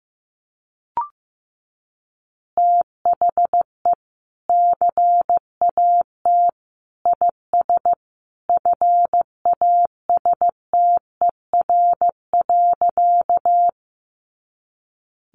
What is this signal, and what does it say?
Morse code, heard as a keyed tone, about 15 wpm, 700 Hz. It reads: THE CAT IS FASTER.
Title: Additional Easy Phrases 15wpm